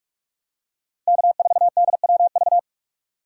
Morse code, heard as a keyed tone, about 44 wpm, 700 Hz. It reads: K4DWV